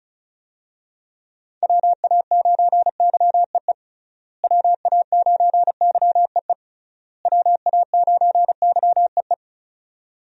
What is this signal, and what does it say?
Morse code, heard as a keyed tone, about 36 wpm, 700 Hz. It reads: WA9YEE WA9YEE WA9YEE
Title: USA Call Signs - 6 Characters - Repeated 3x - 35wpm